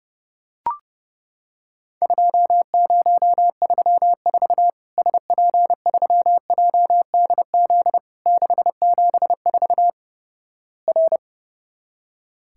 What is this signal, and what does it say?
Morse code, heard as a keyed tone, about 31 wpm, 700 Hz. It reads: 2034 SP3JDZ 674 R
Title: Worked All Europe DX Contest - QTC Exchange - 30wpm